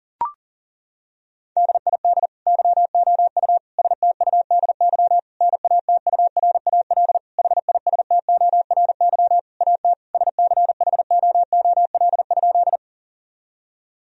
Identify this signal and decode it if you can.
Morse code, heard as a keyed tone, about 40 wpm, 700 Hz. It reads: DID YOU STUDY NATURAL HISTORY AT SCHOOL?
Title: Sentences from Top 1000 Words - Review Part 4 - 40wpm